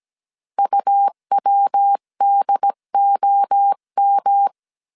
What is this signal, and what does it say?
Morse code, heard as a keyed tone, about 17 wpm, 800 Hz. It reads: UWDOM
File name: CW / DET / 1 3 1500C S A 16wpm 2141